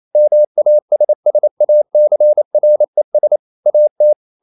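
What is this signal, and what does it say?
Morse code, heard as a keyed tone, about 28 wpm, 600 Hz. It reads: MASSACRES AT
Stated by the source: Morse code practice at 28 wpm, text of Ken Follett's novel Never in English, from CW med Gnister.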